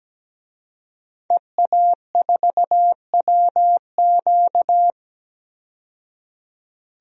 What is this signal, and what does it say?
Morse code, heard as a keyed tone, about 17 wpm, 700 Hz. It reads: EA4WQ
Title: DXCC-100 Call Signs - 5 Characters - 17wpm